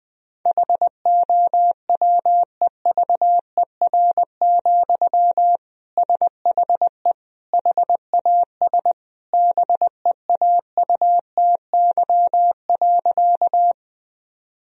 Morse code, heard as a keyed tone, about 20 wpm, 700 Hz. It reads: HOWEVER, SHE HAS BEAUTY.
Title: Sentences from Top 1000 Words - Review Part 2 - 20wpm